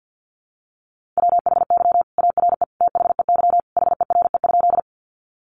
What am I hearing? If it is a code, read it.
Morse code, heard as a keyed tone, about 50 wpm, 700 Hz. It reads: WHY ARE THEY HERE?
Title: Sentences from Top 200 Words 50wpm